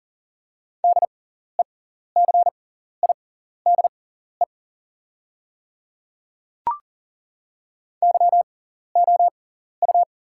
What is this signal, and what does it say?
Morse code, heard as a keyed tone, about 40 wpm, spaced as for 14 wpm, 700 Hz. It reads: DECIDE YOU
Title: 6x Character Spacing - Top 500 Words - 40wpm